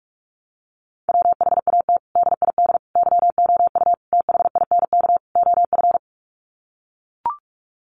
Morse code, heard as a keyed tone, about 44 wpm, 700 Hz. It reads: WHAT DID YOU THINK OF
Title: Sets of 5 Words Encore2 45wpm